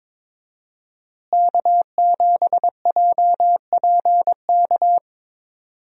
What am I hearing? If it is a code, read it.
Morse code, heard as a keyed tone, about 22 wpm, 700 Hz. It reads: K7JPK
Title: USA Call Signs - 5 Characters - Repeated 2x - 22wpm